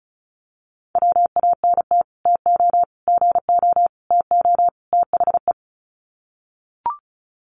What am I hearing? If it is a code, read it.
Morse code, heard as a keyed tone, about 34 wpm, 700 Hz. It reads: WANT TO GO TO THE
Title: Sets of 5 Words Encore2 35wpm